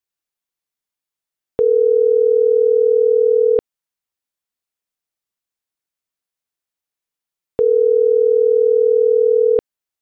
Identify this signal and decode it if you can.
Ringback tone (North America), 440 + 480 Hz.
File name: CENTRAL AGUILERA / out ram.lmorena.2025 - .19211